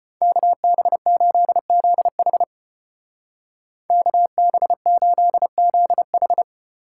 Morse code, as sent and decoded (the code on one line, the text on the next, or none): -.- -... ---.. --.. .... / -.- -... ---.. --.. ....
KB8ZH KB8ZH